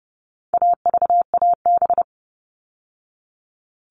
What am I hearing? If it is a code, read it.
Morse code, heard as a keyed tone, about 30 wpm, 700 Hz. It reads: AVAB
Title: All Cut Numbers - 4-Digit 30wpm